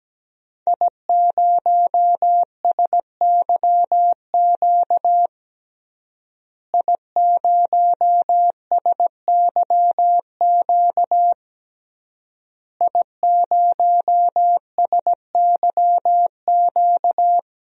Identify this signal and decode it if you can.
Morse code, heard as a keyed tone, about 17 wpm, 700 Hz. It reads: I0SYQ I0SYQ I0SYQ
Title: DXCC-100 Call Signs - 5 Characters - Repeated 3x - 17wpm